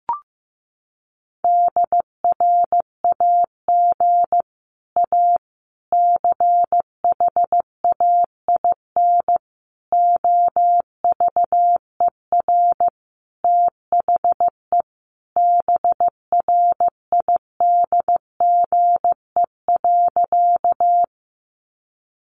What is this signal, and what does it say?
Morse code, heard as a keyed tone, about 15 wpm, 700 Hz. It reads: DRAG A CHAIN OVER THE BRIDGE.
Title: Sentences from Top 1500 Words 15wpm